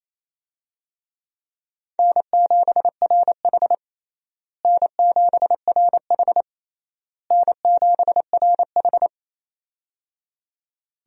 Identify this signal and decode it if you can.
Morse code, heard as a keyed tone, about 28 wpm, 700 Hz. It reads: N7RH N7RH N7RH
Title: USA Call Signs - 4 Characters - Repeated 3x - 28wpm